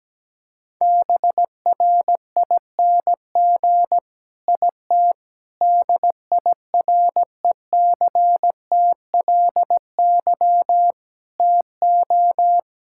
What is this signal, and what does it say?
Morse code, heard as a keyed tone, about 17 wpm, 700 Hz. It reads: BRING IT DIRECTLY TO
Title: Sentences from Top 1000 Words - Review Part 5 - 17wpm